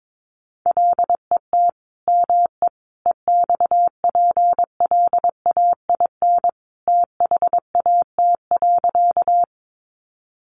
Morse code, heard as a keyed tone, about 22 wpm, 700 Hz. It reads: LET ME EXPLAIN THAT.
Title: Sentences from Top 1100 Words 22wpm